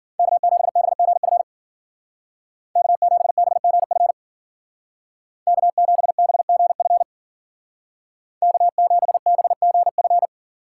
Morse code, heard as a keyed tone, about 48 wpm, 700 Hz. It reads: K7BGF K7BGF K7BGF K7BGF
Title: Speed Racing - USA Call Signs 40wpm